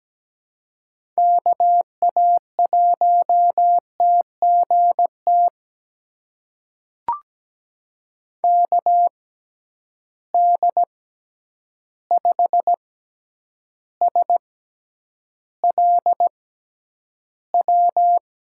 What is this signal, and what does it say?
Morse code, heard as a keyed tone, about 17 wpm, spaced as for 9 wpm, 700 Hz. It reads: KA1TGT KD5SLW
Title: USA Call Signs - 6 Characters - 6x Character Spacing - 17wpm